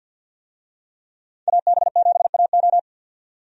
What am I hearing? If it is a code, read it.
Morse code, heard as a keyed tone, about 52 wpm, 700 Hz. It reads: AB7AO
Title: USA Call Signs - 5 Characters - 50wpm